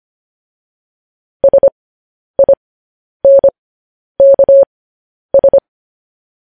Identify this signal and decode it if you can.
Morse code, heard as a keyed tone, about 25 wpm, spaced as for 10 wpm, 550 Hz. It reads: SINKS
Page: news-headlines-in-morse-code-at-10-wpm